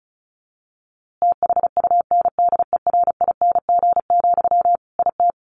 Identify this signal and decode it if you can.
Morse code, heard as a keyed tone, about 34 wpm, 700 Hz. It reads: THUNDERING, IT
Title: A Sound Of Thunder 35wpm